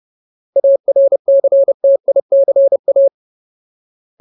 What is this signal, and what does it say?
Morse code, heard as a keyed tone, about 30 wpm, 550 Hz. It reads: ARCTICA
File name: Ultimate Word List / 30WPM Ultimate